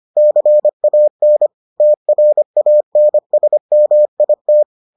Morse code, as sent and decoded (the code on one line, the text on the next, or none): -.-. .- -. / - .-. .- -. ... -- .. -
CAN TRANSMIT